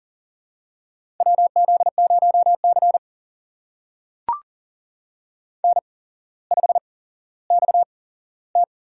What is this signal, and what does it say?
Morse code, heard as a keyed tone, about 40 wpm, spaced as for 17 wpm, 700 Hz. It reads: WZ0C N5XT
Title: USA Call Signs - 4 Characters - 8x Character Spacing - 40wpm